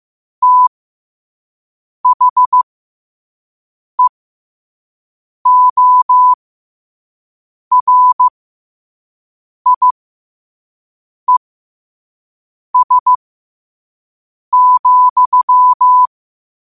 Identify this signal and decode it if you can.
Morse code, heard as a keyed tone, about 15 wpm, spaced as for 5 wpm, 1000 Hz. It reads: THEORIES,
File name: qotdfiles / 5wpm